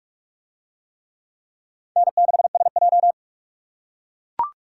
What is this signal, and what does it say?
Morse code, heard as a keyed tone, about 42 wpm, 700 Hz. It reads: N6SJ